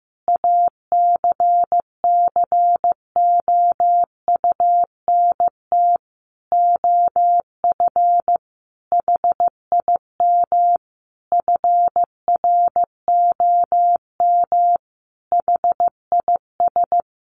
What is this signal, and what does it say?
Morse code, heard as a keyed tone, about 15 wpm, 700 Hz. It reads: ACCOUNT OF HIM FROM HIS
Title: Sentences from Top 900 Words 15wpm